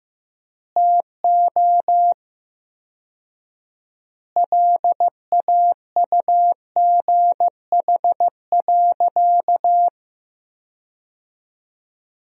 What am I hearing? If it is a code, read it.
Morse code, heard as a keyed tone, about 15 wpm, 700 Hz. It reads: TO LAUGH.
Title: 4x Word-Spacing - Sentences from Top 500 Words - 15wpm